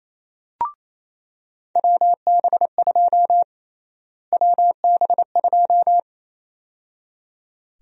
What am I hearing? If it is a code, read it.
Morse code, heard as a keyed tone, about 28 wpm, 700 Hz. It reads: WB2 WB2
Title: USA Call Sign Prefixes - Repeated 2x - 28wpm